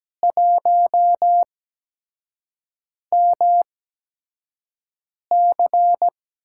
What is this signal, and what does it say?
Morse code, heard as a keyed tone, about 17 wpm, spaced as for 5 wpm, 700 Hz. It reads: 1MC
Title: USA Call Signs - 8x Character Spacing - 17wpm